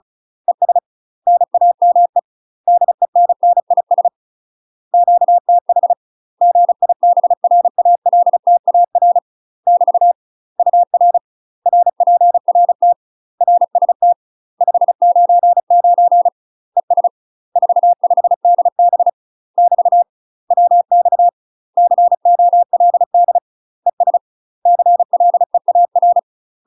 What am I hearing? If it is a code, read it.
Morse code, heard as a keyed tone, about 36 wpm, 700 Hz. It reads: ES NAME DENNIS QTH GIBRALTAR <BT> UR RPRT RST 599 ES 45DB <BT> WX COLD ES CLEAR
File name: UK MASTER QSOs TRAINING / 35WPM / UK MASTER QSOs 0023